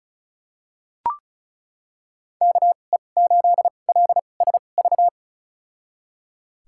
Morse code, heard as a keyed tone, about 36 wpm, spaced as for 25 wpm, 700 Hz. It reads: KE8LSV
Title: USA Call Signs - 6 Characters - 2x Character Spacing - 35wpm